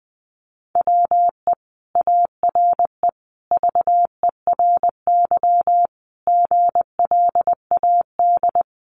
Morse code, heard as a keyed tone, about 20 wpm, 700 Hz. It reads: WE ARE VERY GLAD
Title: Sentences from Top 800 Words 20wpm